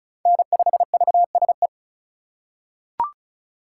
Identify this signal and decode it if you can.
Morse code, heard as a keyed tone, about 34 wpm, 700 Hz. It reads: N5VSE
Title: USA Call Signs - 5 Characters - 3x Character Spacing - 35wpm